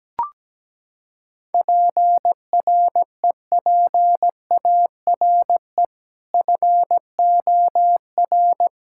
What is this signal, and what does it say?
Morse code, heard as a keyed tone, about 17 wpm, 700 Hz. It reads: PREPARE FOR
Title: Sentences from Top 1000 Words - Review Part 6 - 17wpm